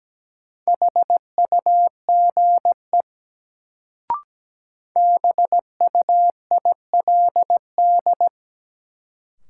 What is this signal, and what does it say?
Morse code, heard as a keyed tone, about 17 wpm, 700 Hz. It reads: HUGE BUILD
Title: Sentences from Top 1100 Words - New Word Drill Alt - 17wpm